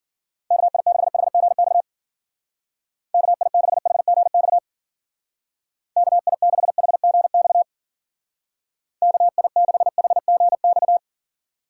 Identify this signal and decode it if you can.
Morse code, heard as a keyed tone, about 50 wpm, 700 Hz. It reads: KI6HGX KI6HGX KI6HGX KI6HGX